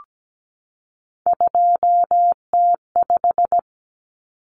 Morse code, heard as a keyed tone, about 17 wpm, 700 Hz. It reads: 2T5
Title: Common Cut Numbers - 3-Digit 17wpm